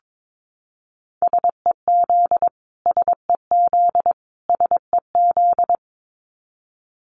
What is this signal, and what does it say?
Morse code, heard as a keyed tone, about 22 wpm, 700 Hz. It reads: SEZ SEZ SEZ
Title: Abbreviations Repeated 3x - 22wpm